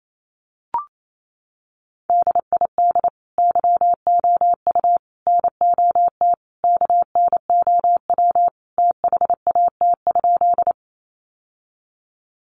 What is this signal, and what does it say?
Morse code, heard as a keyed tone, about 28 wpm, 700 Hz. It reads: DID YOU NOT KNOW THAT?
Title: Sentences from Top 600 Words 28wpm